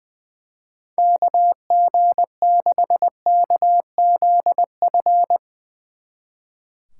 Morse code, heard as a keyed tone, about 22 wpm, 700 Hz. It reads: KG6KZF
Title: Speed Racing - USA Call Signs 20wpm